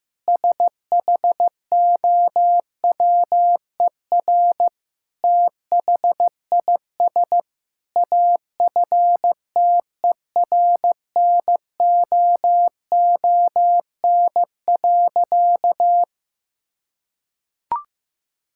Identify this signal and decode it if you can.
Morse code, heard as a keyed tone, about 14 wpm, 700 Hz. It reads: SHOWER THIS AFTERNOON.
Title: Sentences from Top 1600 Words 15wpm